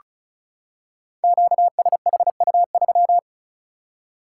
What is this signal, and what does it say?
Morse code, heard as a keyed tone, about 36 wpm, 700 Hz. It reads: QSHU3
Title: Code Groups - 5 Letter:Numbers - 35wpm